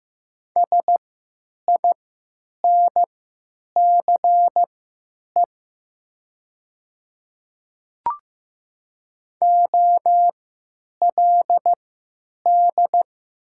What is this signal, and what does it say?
Morse code, heard as a keyed tone, about 15 wpm, spaced as for 9 wpm, 700 Hz. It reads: SINCE OLD
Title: 3x Character Spacing - Top 300 Words - 15wpm